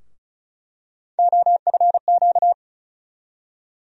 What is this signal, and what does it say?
Morse code, heard as a keyed tone, about 36 wpm, 700 Hz. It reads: OFQ
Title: USA Call Sign Postfixes - 3 Letters 35wpm